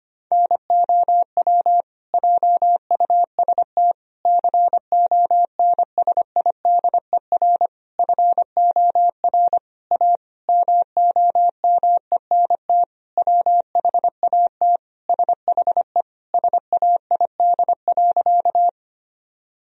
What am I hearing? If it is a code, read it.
Morse code, heard as a keyed tone, about 25 wpm, 700 Hz. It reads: NOW JUST CONSIDER FOR A MOMENT WHAT SHE SAID.
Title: Sentences from Top 1000 Words - Review Part 4 - 25wpm